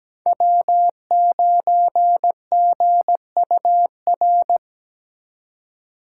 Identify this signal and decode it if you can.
Morse code, heard as a keyed tone, about 20 wpm, 700 Hz. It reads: W9GUR
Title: Speed Racing - USA 1x3 Call Signs 17wpm